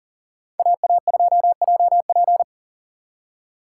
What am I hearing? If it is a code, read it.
Morse code, heard as a keyed tone, about 52 wpm, 700 Hz. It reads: AA2JP